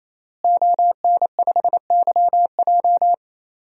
Morse code, heard as a keyed tone, about 28 wpm, 700 Hz. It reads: ON5YJ
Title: DXCC-100 Call Signs - 5 Characters - Repeated 3x - 28wpm